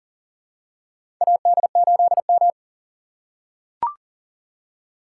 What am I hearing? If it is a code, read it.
Morse code, heard as a keyed tone, about 40 wpm, 690 Hz. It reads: AD8M
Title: USA Call Signs - 4 Characters - 2x Character Spacing - 40wpm